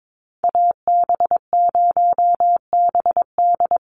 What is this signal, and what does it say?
Morse code, heard as a keyed tone, about 22 wpm, 700 Hz. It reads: AB0BD
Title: USA Call Signs Repeated 2x 22wpm